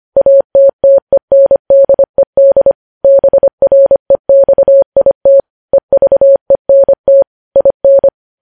Morse code, heard as a keyed tone, about 25 wpm, 550 Hz. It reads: ATTENDED BREXIT EVENT IN
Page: news-headlines-in-morse-code-at-25-wpm